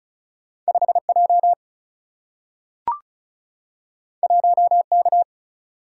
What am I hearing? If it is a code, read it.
Morse code, heard as a keyed tone, about 36 wpm, 700 Hz. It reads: 5J 1K
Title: Single-Letter Single-Number 35wpm